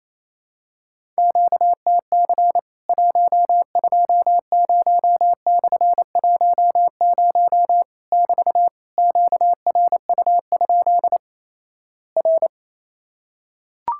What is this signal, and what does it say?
Morse code, heard as a keyed tone, about 28 wpm, 700 Hz. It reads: QTC 120/10 <BT> QRU? R
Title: Worked All Europe DX Contest - QTC Exchange - 28wpm